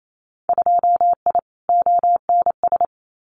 Morse code, heard as a keyed tone, about 28 wpm, 700 Hz. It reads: 2I ONS
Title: Winter Field Day Exchanges Encore 28wpm